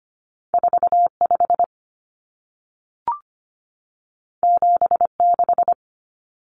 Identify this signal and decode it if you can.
Morse code, heard as a keyed tone, about 25 wpm, 700 Hz. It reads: 45 76
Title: Numbers - 2-Digit 25wpm